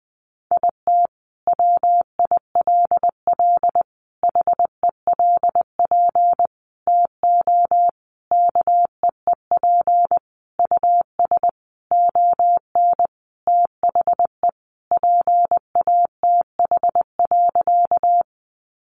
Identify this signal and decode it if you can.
Morse code, heard as a keyed tone, about 19 wpm, 700 Hz. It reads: IT WILL HELP TO KEEP US ON THE PATH.
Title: Sentences from Top 1400 Words 20wpm